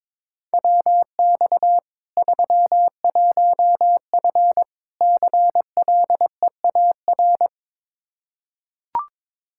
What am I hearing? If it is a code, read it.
Morse code, heard as a keyed tone, about 22 wpm, 700 Hz. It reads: WX 31F CLEAR
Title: Practice QSO - Exchange 2 - Info - 22wpm